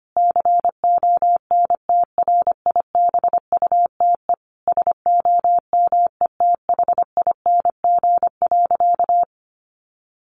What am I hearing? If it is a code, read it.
Morse code, heard as a keyed tone, about 25 wpm, 700 Hz. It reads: CONTRIBUTE SOMETHING.
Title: Sentences from Top 1400 Words 25wpm